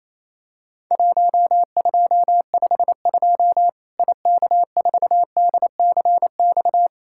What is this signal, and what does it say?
Morse code, heard as a keyed tone, about 28 wpm, 700 Hz. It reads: 1252 IK4DCX
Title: Worked All Europe DX Contest - QTC Exchange - 28wpm